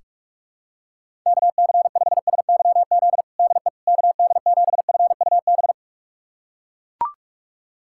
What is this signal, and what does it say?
Morse code, heard as a keyed tone, about 41 wpm, 700 Hz. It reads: KC5SYZ DE KD7FUB
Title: USA Callsign DE Callsign - 6 Characters - 45wpm